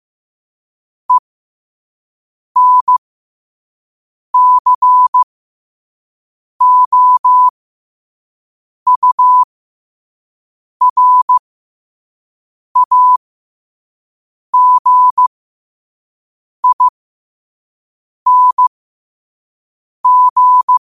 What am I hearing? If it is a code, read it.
Morse code, heard as a keyed tone, about 15 wpm, spaced as for 5 wpm, 1000 Hz. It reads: ENCOURAGING
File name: qotdfiles / 5wpm